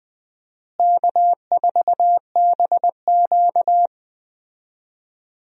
Morse code, heard as a keyed tone, about 20 wpm, 700 Hz. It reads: K4BQ
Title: USA Call Signs - 4 Characters - Repeated 2x - 20wpm